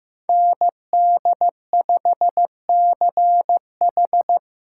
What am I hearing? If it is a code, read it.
Morse code, heard as a keyed tone, about 18 wpm, 700 Hz. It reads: ND5CH
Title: Speed Racing - USA 2x2 Call Signs 15wpm